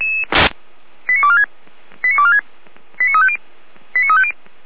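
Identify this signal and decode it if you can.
Five-tone selective call (ZVEI), heard as quick runs of five tones, tones of about 70 ms. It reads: 89246 89246 89350 89350
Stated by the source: ZVEI five-tone call 89240 received by SDR, from audio.f85360.feuerwehr-orlamuende.org, fire brigade Orlamünde.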